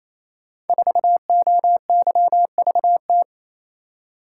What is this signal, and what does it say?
Morse code, heard as a keyed tone, about 27 wpm, 700 Hz. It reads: 4OYVT
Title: Code Groups - 5 Letter:Numbers - 28wpm